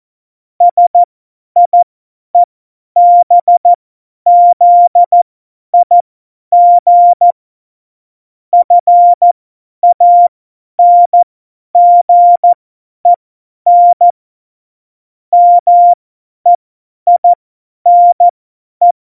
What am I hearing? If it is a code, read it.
Morse code, heard as a keyed tone, about 14 wpm, spaced as for 10 wpm, 700 Hz. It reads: SIEBZIG FANGEN MEINE
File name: Morsekurs / Witze / Witz 14 Juli 2020, 50 F70